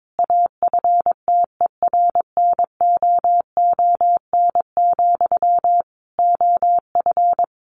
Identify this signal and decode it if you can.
Morse code, heard as a keyed tone, about 22 wpm, 700 Hz. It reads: AFTERNOON, OF
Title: Sentences from Top 1500 Words 22wpm